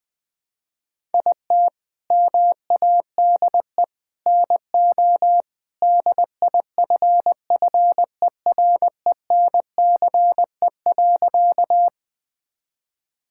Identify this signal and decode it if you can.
Morse code, heard as a keyed tone, about 20 wpm, 700 Hz. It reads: IT MADE NO DIFFERENCE.
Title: Sentences from Top 1000 Words - Review Part 5 - 20wpm